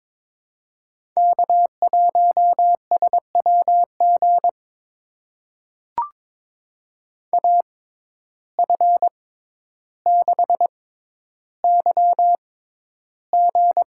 Morse code, heard as a keyed tone, about 23 wpm, spaced as for 11 wpm, 700 Hz. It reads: K1SWG AF6YG
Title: USA Call Signs - 5 Characters - 6x Character Spacing - 22wpm